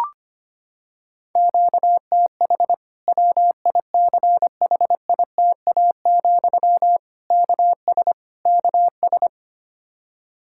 Morse code, heard as a keyed tone, about 25 wpm, 700 Hz. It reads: QTH WICHITA, KS KS
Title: K1USN SST QSO Practice 25wpm